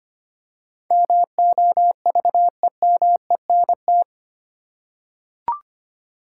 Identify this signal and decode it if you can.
Morse code, heard as a keyed tone, about 25 wpm, 700 Hz. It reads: MOVEMENT
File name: 8 Letter Words - First 3 Letters Given - 25wpm